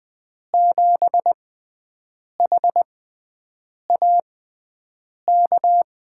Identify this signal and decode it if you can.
Morse code, heard as a keyed tone, about 20 wpm, spaced as for 7 wpm, 700 Hz. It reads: 7HAK
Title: USA Call Signs - 6 Characters - 6x Character Spacing - 20wpm